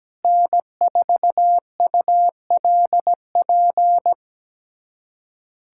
Morse code, heard as a keyed tone, about 17 wpm, 700 Hz. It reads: N4ULP
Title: USA Callsign DE Callsign - 5 Characters - 17wpm